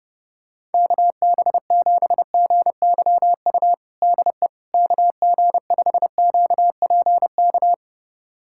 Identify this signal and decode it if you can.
Morse code, heard as a keyed tone, about 32 wpm, 700 Hz. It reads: KB7GYU DE KG5QPK
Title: USA Callsign DE Callsign - 6 Characters - 30wpm